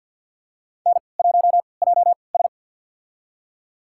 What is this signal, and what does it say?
Morse code, heard as a keyed tone, about 53 wpm, spaced as for 29 wpm, 700 Hz. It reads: N1JS
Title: USA Call Signs - 3x Character Spacing - 50wpm